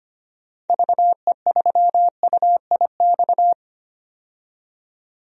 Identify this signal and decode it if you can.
Morse code, heard as a keyed tone, about 25 wpm, 700 Hz. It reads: VE3UIX
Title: DXCC-100 Call Signs - 6 Characters - 25wpm